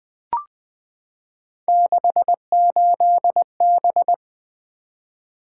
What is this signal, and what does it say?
Morse code, heard as a keyed tone, about 20 wpm, 700 Hz. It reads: 68B
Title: Three Letter-Number Combinations - 20wpm